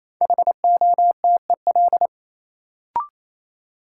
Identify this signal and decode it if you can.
Morse code, heard as a keyed tone, about 28 wpm, 700 Hz. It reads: HOTEL
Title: Sentences from Top 1600 Words - New Word Drill Alt - 28wpm